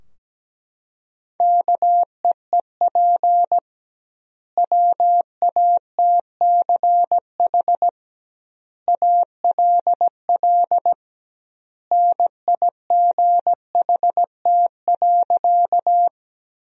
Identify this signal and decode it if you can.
Morse code, heard as a keyed tone, about 17 wpm, 700 Hz. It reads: KEEP WATCH ALL NIGHT.
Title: 2x Word-Spacing - Sentences from Top 300 Words - 17wpm